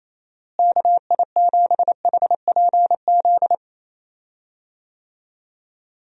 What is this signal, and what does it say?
Morse code, heard as a keyed tone, about 28 wpm, 700 Hz. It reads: KI7HPZ
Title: USA Call Signs - 6 Characters - Repeated 2x - 28wpm